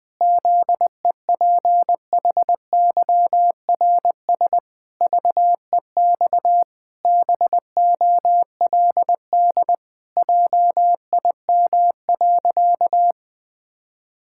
Morse code, heard as a keyed tone, about 21 wpm, 700 Hz. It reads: ZEPHYRS VEX BOLD JIM.
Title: Pangram Challenge v2 - 20wpm